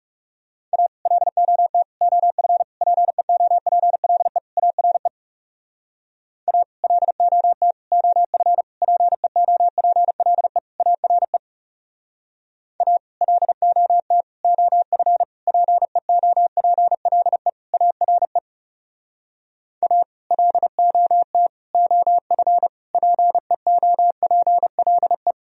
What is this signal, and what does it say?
Morse code, heard as a keyed tone, about 39 wpm, 700 Hz. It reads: A LOT OF PEOPLE ARE A LOT OF PEOPLE ARE A LOT OF PEOPLE ARE A LOT OF PEOPLE